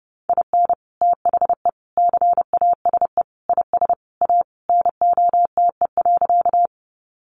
Morse code, heard as a keyed tone, about 30 wpm, 700 Hz. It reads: IN THE CASE IS A NOTE.
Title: Sentences from Top 700 Words 30wpm